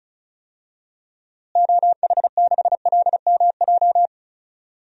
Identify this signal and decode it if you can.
Morse code, heard as a keyed tone, about 34 wpm, 700 Hz. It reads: OH6LMJ
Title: DXCC-100 Call Signs - 6 Characters - Repeated 3x - 35wpm